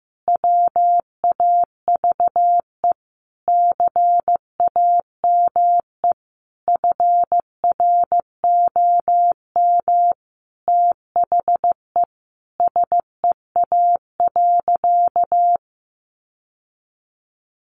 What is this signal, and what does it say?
Morse code, heard as a keyed tone, about 15 wpm, 700 Hz. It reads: WAVE CAME FROM THE SEA.
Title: Sentences from Top 1400 Words 15wpm